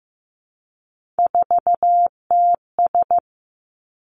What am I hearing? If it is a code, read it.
Morse code, heard as a keyed tone, about 15 wpm, 700 Hz. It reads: 4TS
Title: Three Challenging Letter-Number Combinations 15wpm